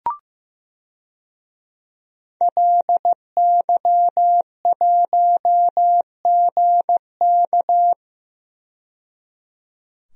Morse code, heard as a keyed tone, about 15 wpm, 700 Hz. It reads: LY1GK
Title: DXCC-100 Call Signs - 5 Characters - 15wpm